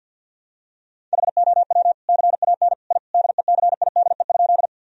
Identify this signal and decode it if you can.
Morse code, heard as a keyed tone, about 51 wpm, 700 Hz. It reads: HOW CAN I DECIDE?
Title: Sentences from Top 1000 Words - Review Part 3 - 50wpm